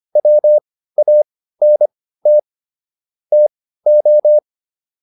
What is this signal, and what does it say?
Morse code, heard as a keyed tone, about 25 wpm, spaced as for 15 wpm, 600 Hz. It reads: WANT TO
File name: WD405 15